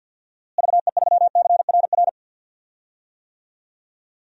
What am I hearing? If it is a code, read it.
Morse code, heard as a keyed tone, about 46 wpm, 700 Hz. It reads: VE3CRR
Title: DXCC-100 Call Signs - 6 Characters - Repeated 3x - 50wpm